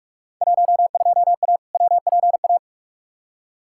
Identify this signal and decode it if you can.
Morse code, heard as a keyed tone, about 46 wpm, 700 Hz. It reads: 12A WPA
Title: Field Day Exchanges Encore - 45wpm